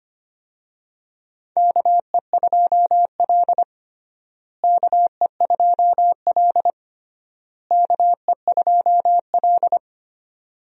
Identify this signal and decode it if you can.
Morse code, heard as a keyed tone, about 25 wpm, 700 Hz. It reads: KE2L KE2L KE2L
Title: USA Call Signs - 4 Characters - Repeated 3x - 25wpm